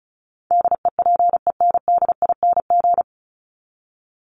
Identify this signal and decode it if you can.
Morse code, heard as a keyed tone, about 34 wpm, 700 Hz. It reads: DEPENDING
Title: ING Suffix - Top 100 Words with Base Words 35wpm